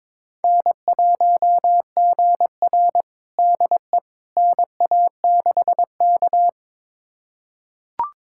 Morse code, heard as a keyed tone, about 22 wpm, 700 Hz. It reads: N1GR DE NA6K
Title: USA Callsign DE Callsign - 4 Characters - 22wpm